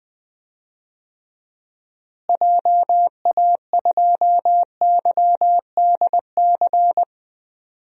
Morse code, heard as a keyed tone, about 20 wpm, 700 Hz. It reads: JA2YDC
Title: DXCC-100 Call Signs - 6 Characters - Repeated 3x - 20wpm